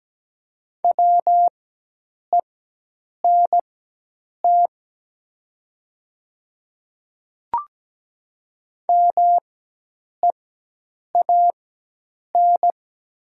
Morse code, heard as a keyed tone, about 17 wpm, spaced as for 8 wpm, 700 Hz. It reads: WENT MEAN
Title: 4x Character Spacing - Top 200 Words - 17wpm